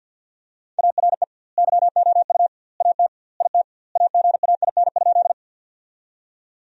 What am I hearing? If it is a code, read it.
Morse code, heard as a keyed tone, about 50 wpm, 700 Hz. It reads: ARE YOU AT IT AGAIN?
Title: 2x Word-Spacing - Sentences from Top 200 Words - 50wpm